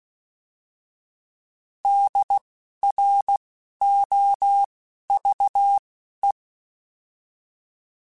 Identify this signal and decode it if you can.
Morse code, heard as a keyed tone, about 16 wpm, spaced as for 12 wpm, 800 Hz. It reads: DROVE
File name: Alphabetical / Alphabetical mp3s / Ds @10